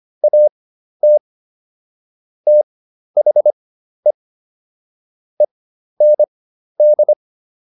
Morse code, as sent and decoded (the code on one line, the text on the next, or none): .- - / - .... / . -. -..
AT THE END